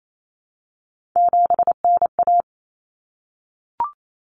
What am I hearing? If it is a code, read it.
Morse code, heard as a keyed tone, about 28 wpm, 700 Hz. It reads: 7NA
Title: Common Cut Numbers - 3-Digit 28wpm